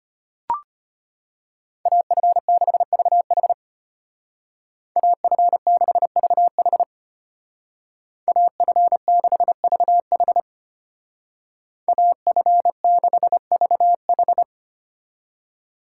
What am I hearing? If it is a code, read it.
Morse code, heard as a keyed tone, about 30 wpm, 700 Hz. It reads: AF6VH AF6VH AF6VH AF6VH
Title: Speed Racing - USA 2x2 Call Signs 25wpm